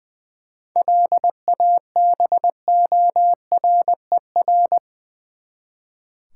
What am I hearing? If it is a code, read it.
Morse code, heard as a keyed tone, about 19 wpm, 700 Hz. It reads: LABORER
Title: K1USN SST QSO Practice 20wpm